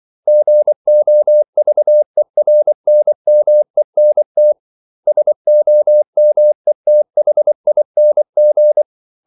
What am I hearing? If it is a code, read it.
Morse code, heard as a keyed tone, about 24 wpm, 600 Hz. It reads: GOVERNMENT SOMETHING
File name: CW / Practice / 500MostCommonEnglishWords24